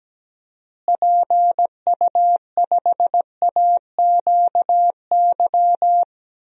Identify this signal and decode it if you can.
Morse code, heard as a keyed tone, about 17 wpm, 700 Hz. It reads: PU5AQY
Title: DXCC-100 Call Signs - 6 Characters - Repeated 2x - 17wpm